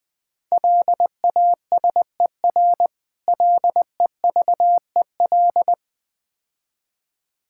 Morse code, heard as a keyed tone, about 20 wpm, 690 Hz. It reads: LASER LEVEL
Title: Difficult Words - Speed-Racing with Hints - 20wpm